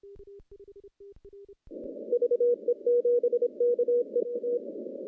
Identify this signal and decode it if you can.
Morse code, heard as a keyed tone, about 26 wpm, 480 Hz. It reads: VE7KEA
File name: contests / CW-SS-k5tr- -0400